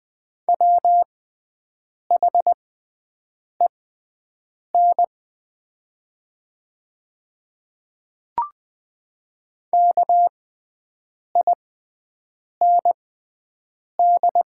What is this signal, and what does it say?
Morse code, heard as a keyed tone, about 20 wpm, spaced as for 7 wpm, 700 Hz. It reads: WHEN KIND
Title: 6x Character Spacing - Top 200 Words - 20wpm